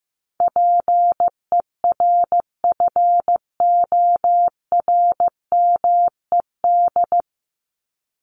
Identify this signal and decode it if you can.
Morse code, heard as a keyed tone, about 15 wpm, 700 Hz. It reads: PERFORMED
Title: ED Suffix - Top 100 Words with Base Words 15wpm